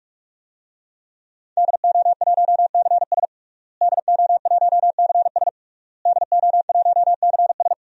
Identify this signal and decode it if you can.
Morse code, heard as a keyed tone, about 46 wpm, 700 Hz. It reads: DO1CS DO1CS DO1CS
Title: DXCC-100 Call Signs - 5 Characters - Repeated 3x - 45wpm